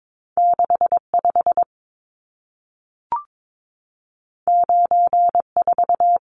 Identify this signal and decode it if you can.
Morse code, heard as a keyed tone, about 22 wpm, 700 Hz. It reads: 65 94
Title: Numbers - 2-Digit 22wpm